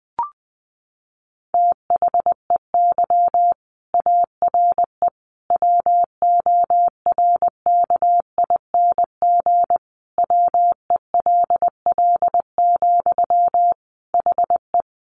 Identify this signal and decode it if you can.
Morse code, heard as a keyed tone, about 20 wpm, 700 Hz. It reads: THEY ARE WORKING WELL, HE
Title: Sentences from Top 700 Words 20wpm